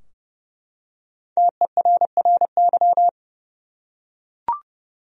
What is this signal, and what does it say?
Morse code, heard as a keyed tone, about 30 wpm, 700 Hz. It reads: TERRY
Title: USA Common Names - 30wpm